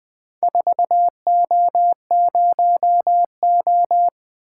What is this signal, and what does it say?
Morse code, heard as a keyed tone, about 20 wpm, 700 Hz. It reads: 4O0O
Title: Four Letter-Number Combinations - 20wpm